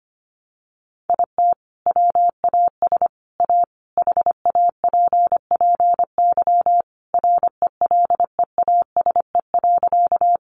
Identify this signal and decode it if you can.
Morse code, heard as a keyed tone, about 25 wpm, 700 Hz. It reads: IT WAS A HAPPY RELEASE.
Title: Sentences from Top 1100 Words 25wpm